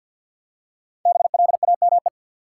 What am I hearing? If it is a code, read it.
Morse code, heard as a keyed tone, about 47 wpm, 700 Hz. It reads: BLAME
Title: Five Letter Words - 50wpm